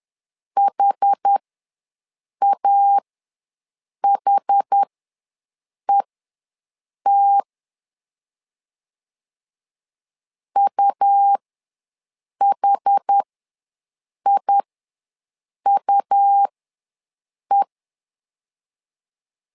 Morse code, heard as a keyed tone, about 10 wpm, spaced as for 6 wpm, 800 Hz. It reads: HAHET UHIUE
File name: LEZIONI / 2 UHDN / 2 UH ws 10wpm 2029